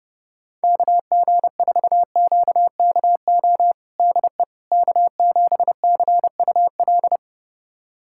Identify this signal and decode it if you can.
Morse code, heard as a keyed tone, about 30 wpm, 700 Hz. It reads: KG4QKO DE K7CUL
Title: USA Callsign DE Callsign 30wpm